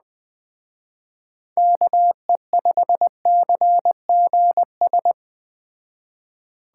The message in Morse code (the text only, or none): KE5CGS